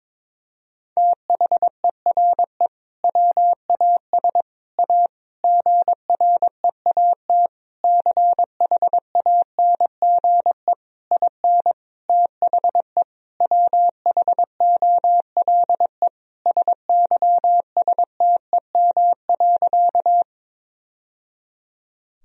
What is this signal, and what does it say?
Morse code, heard as a keyed tone, about 22 wpm, 700 Hz. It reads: THERE WAS A GREAT CHANGE IN THE WHOLE SYSTEM.
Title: Sentences from Top 1000 Words - Review Part 3 - 22wpm